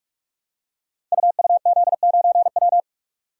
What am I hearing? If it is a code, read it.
Morse code, heard as a keyed tone, about 44 wpm, 700 Hz. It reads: UUZ9W
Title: Code Groups - 5 Letter:Numbers - 45wpm